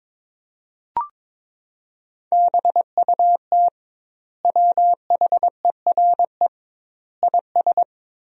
Morse code, heard as a keyed tone, about 22 wpm, 700 Hz. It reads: BUT WHERE IS
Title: 2x Word-Spacing - Sentences from Top 200 Words - 22wpm